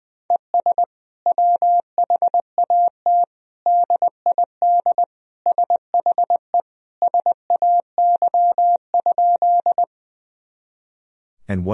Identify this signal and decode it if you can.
Morse code, heard as a keyed tone, about 21 wpm, 700 Hz. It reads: ES WHAT DID SHE SAY?
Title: Sentences from Top 200 Words and Abbreviations 20wpm